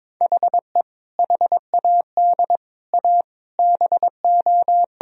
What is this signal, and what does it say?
Morse code, heard as a keyed tone, about 21 wpm, 700 Hz. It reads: HE HAD A BO
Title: Sentences from Top 1000 Words - Review Part 3 - 22wpm